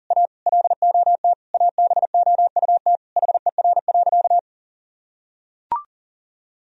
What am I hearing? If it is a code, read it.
Morse code, heard as a keyed tone, about 40 wpm, 700 Hz. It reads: A LOT ABOUT HER.
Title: Sentences from Top 1000 Words - Review Part 3 - 40wpm